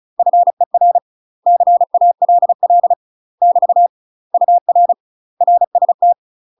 Morse code, heard as a keyed tone, about 34 wpm, 700 Hz. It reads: FER CALL <BT> UR RST
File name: UK MASTER QSOs TRAINING / 35WPM / UK MASTER QSOs 0041